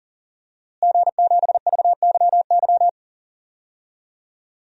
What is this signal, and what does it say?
Morse code, heard as a keyed tone, about 40 wpm, 700 Hz. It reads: G7VYY
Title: DXCC-100 Call Signs - 5 Characters - Repeated 2x - 40wpm